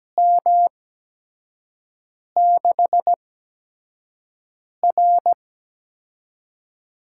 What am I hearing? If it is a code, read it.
Morse code, heard as a keyed tone, about 17 wpm, spaced as for 5 wpm, 700 Hz. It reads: M6R
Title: USA Call Signs - 6 Characters - 8x Character Spacing - 17wpm